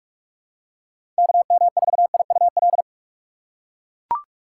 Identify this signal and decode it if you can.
Morse code, heard as a keyed tone, about 43 wpm, 700 Hz. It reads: KM4IUL
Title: USA Call Signs - 6 Characters - 6x Character Spacing - 45wpm